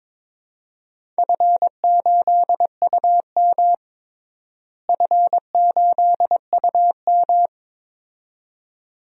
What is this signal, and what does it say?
Morse code, heard as a keyed tone, about 22 wpm, 700 Hz. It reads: F8UM F8UM
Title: DXCC-100 Call Signs - 4 Characters - Repeated 2x - 22wpm